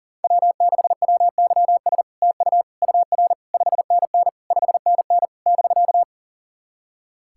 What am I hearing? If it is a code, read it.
Morse code, heard as a keyed tone, about 43 wpm, 700 Hz. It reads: W6WYS TU UR 5NN 5NN <BK>
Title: SOTA QSO Practice 40wpm